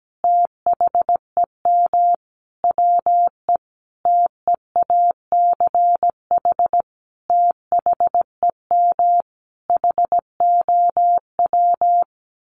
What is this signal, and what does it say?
Morse code, heard as a keyed tone, about 17 wpm, 700 Hz. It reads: THEM WE TEACH THEM HOW
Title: Sentences from Top 1300 Words 17wpm